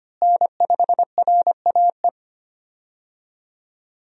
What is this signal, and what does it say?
Morse code, heard as a keyed tone, about 25 wpm, 700 Hz. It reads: N5RAE
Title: USA Call Signs - 5 Characters - Repeated 3x - 25wpm